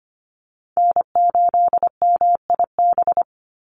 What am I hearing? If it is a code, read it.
Morse code, heard as a keyed tone, about 25 wpm, 700 Hz. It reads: N8MIB
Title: USA Call Signs 25wpm